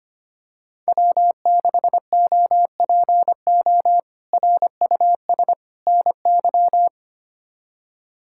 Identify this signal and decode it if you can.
Morse code, heard as a keyed tone, about 25 wpm, 700 Hz. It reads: W6OPO RUS NY
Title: K1USN SST QSO Practice - 25wpm